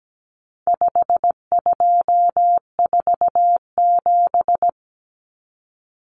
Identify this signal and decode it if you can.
Morse code, heard as a keyed tone, about 17 wpm, 700 Hz. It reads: 5247
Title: Common Cut Numbers - 4-Digit 17wpm